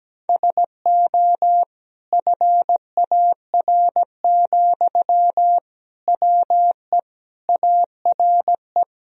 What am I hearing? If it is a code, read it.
Morse code, heard as a keyed tone, about 17 wpm, 700 Hz. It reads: SO FAR, WE ARE
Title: Sentences from Top 1000 Words - Review Part 4 - 17wpm